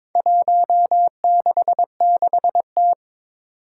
Morse code, heard as a keyed tone, about 22 wpm, 700 Hz. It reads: 166T
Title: Four Challenging Letter-Number Combinations 22wpm